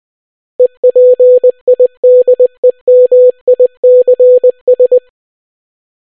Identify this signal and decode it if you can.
Morse code, heard as a keyed tone, about 20 wpm, 500 Hz. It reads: EPIDEMICS